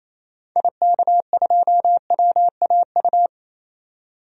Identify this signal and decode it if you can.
Morse code, heard as a keyed tone, about 28 wpm, 700 Hz. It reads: IK2WAU
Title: DXCC-100 Call Signs - 6 Characters - 28wpm